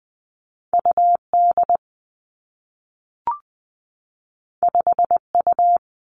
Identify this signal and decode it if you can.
Morse code, heard as a keyed tone, about 20 wpm, 700 Hz. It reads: U D 5 U
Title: Two Challenging Letter-Number Combinations 20wpm